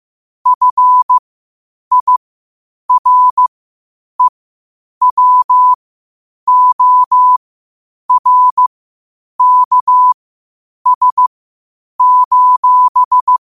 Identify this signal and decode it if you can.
Morse code, heard as a keyed tone, about 15 wpm, spaced as for 8 wpm, 1000 Hz. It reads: FIREWORKS:
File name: qotdfiles / 8wpm